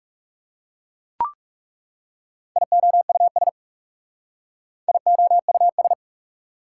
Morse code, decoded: IOUS IOUS

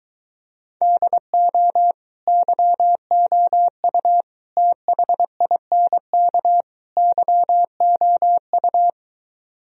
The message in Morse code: DO YOU THINK YOU

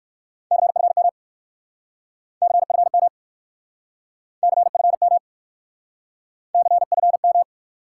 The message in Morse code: CFM CFM CFM CFM

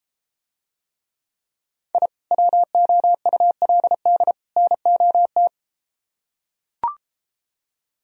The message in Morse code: I WOULD NOT

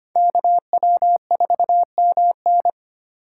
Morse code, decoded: KW4MN

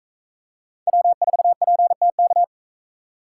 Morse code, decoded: W4PTK

